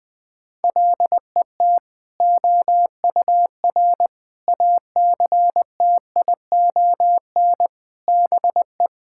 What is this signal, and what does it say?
Morse code, heard as a keyed tone, about 20 wpm, 700 Hz. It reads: LET OUR ACTION BE
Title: Sentences from Top 1000 Words - Review Part 6 - 20wpm